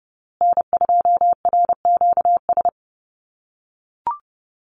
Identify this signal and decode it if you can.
Morse code, heard as a keyed tone, about 28 wpm, 700 Hz. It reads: N2RQS
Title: USA Call Signs Repeated 2x 30wpm